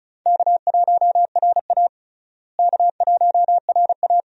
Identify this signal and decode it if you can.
Morse code, heard as a keyed tone, about 34 wpm, 700 Hz. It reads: K1RA K1RA
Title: USA Call Signs - 4 Characters - Repeated 2x - 35wpm